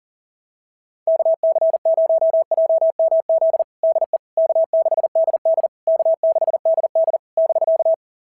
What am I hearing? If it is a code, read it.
Morse code, heard as a keyed tone, about 39 wpm, 650 Hz. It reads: KC0JMZ DE K6DD K6DD <BK>
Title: SOTA QSO Practice 40wpm